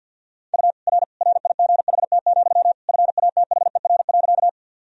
Morse code, heard as a keyed tone, about 48 wpm, 700 Hz. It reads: U R RIGHT, FATHER.
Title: Sentences from Top 300 Words and Abbreviations 50wpm